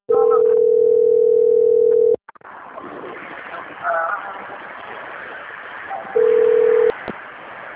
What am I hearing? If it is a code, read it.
Ringback tone (North America), 440 + 480 Hz.
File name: out- - - - - .9435